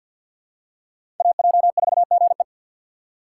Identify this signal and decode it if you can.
Morse code, heard as a keyed tone, about 46 wpm, 700 Hz. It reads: AJ4GE
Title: USA Call Signs - 5 Characters - 4x Character Spacing - 50wpm